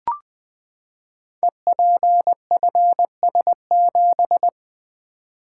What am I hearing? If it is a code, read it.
Morse code, heard as a keyed tone, about 20 wpm, 700 Hz. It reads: EPFS7
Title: Code Groups - 5 Letter:Numbers - 20wpm